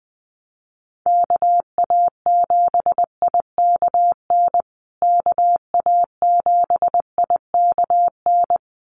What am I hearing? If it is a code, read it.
Morse code, heard as a keyed tone, about 20 wpm, 700 Hz. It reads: KA7IKN KA7IKN
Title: USA Call Signs Repeated 2x 20wpm